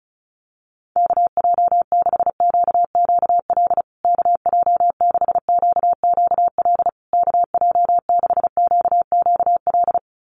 Morse code, heard as a keyed tone, about 34 wpm, 700 Hz. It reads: KJ6QQL KJ6QQL KJ6QQL
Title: USA Call Signs Repeated 3x 35wpm